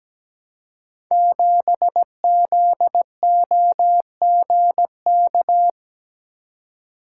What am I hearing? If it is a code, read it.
Morse code, heard as a keyed tone, about 17 wpm, 700 Hz. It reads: 7ZOGK
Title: Code Groups - 5 Letter:Numbers - 17wpm